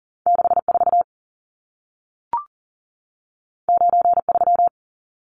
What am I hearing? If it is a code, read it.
Morse code, heard as a keyed tone, about 40 wpm, 700 Hz. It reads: 64 93
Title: Numbers - 2-Digit 40wpm